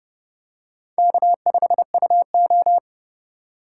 Morse code, heard as a keyed tone, about 29 wpm, 700 Hz. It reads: K5UO